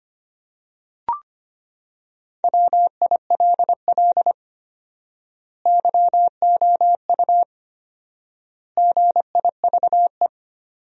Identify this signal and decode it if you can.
Morse code, heard as a keyed tone, about 25 wpm, 700 Hz. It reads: WILL YOU GIVE
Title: 4x Word-Spacing - Sentences from Top 200 Words - 25wpm